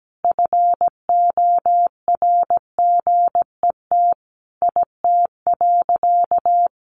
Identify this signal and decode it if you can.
Morse code, heard as a keyed tone, about 17 wpm, 700 Hz. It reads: FORGET IT.
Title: Sentences from Top 1200 Words 17wpm